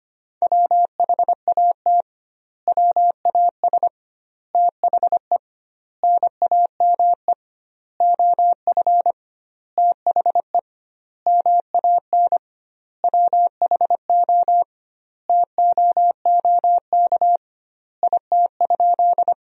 Morse code, heard as a keyed tone, about 25 wpm, 700 Hz. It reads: WHAT WAS THE NAME OF THE MAN WHO TOOK IT?
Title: 2x Word-Spacing - Sentences from Top 300 Words - 25wpm